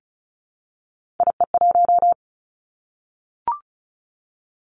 Morse code, heard as a keyed tone, about 36 wpm, 700 Hz. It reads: IE1
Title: Three Challenging Letter-Number Combinations 35wpm